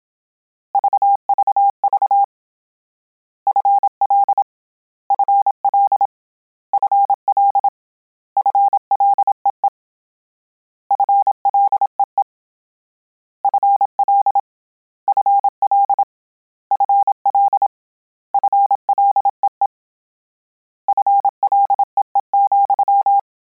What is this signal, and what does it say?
Morse code, heard as a keyed tone, about 27 wpm, 800 Hz. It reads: VVV FL FL FL FLEE FLEE FL FL FL FLEE FLEE,